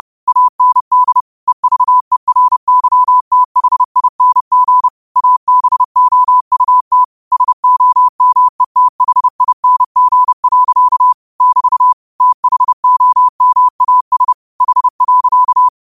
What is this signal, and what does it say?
Morse code, heard as a keyed tone, about 30 wpm, 1000 Hz. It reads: AND EVERYTHING ABOUT SOMETHING. <BT> THOMAS H.
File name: qotdfiles / 30wpm